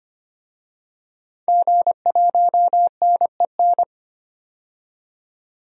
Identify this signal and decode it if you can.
Morse code, heard as a keyed tone, about 25 wpm, 700 Hz. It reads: G1NEN
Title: DXCC-100 Call Signs - 5 Characters - 25wpm